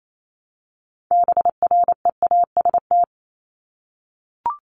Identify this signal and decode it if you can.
Morse code, heard as a keyed tone, about 28 wpm, 700 Hz. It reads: BREAST
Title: Top 1751-2000 Words 28wpm